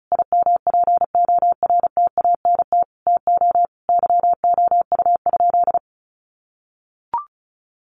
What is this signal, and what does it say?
Morse code, heard as a keyed tone, about 34 wpm, 700 Hz. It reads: IMPORTANT TO YOU?
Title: Sentences from Top 1300 Words 35wpm